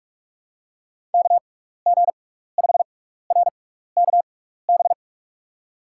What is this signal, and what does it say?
Morse code, heard as a keyed tone, about 46 wpm, spaced as for 16 wpm, 700 Hz. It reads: KG5RKB
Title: USA Call Signs - 6 Characters - 6x Character Spacing - 45wpm